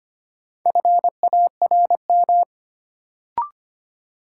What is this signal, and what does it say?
Morse code, heard as a keyed tone, about 25 wpm, 700 Hz. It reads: FARM
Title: Sentences from Top 1000 Words - Review Part 2 - New Word Drill - 25wpm